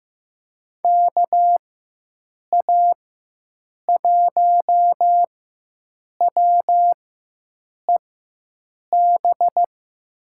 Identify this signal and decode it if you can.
Morse code, heard as a keyed tone, about 15 wpm, spaced as for 7 wpm, 700 Hz. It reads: KA1WEB
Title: USA Call Signs - 6 Characters - 4x Character Spacing - 15wpm